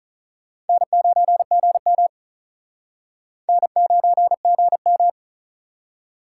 Morse code, heard as a keyed tone, about 48 wpm, 700 Hz. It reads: N9GM N9GM